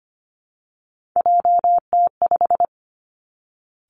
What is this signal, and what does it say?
Morse code, heard as a keyed tone, about 25 wpm, 700 Hz. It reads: JT5
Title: Three Challenging Letter-Number Combinations 25wpm